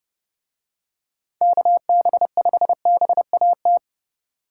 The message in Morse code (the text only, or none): KB5BAT